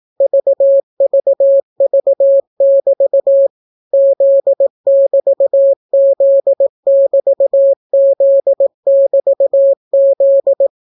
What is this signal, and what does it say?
Morse code, heard as a keyed tone, about 18 wpm, 550 Hz. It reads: VVV<BT> Z<BT>Z<BT>Z<BT>Z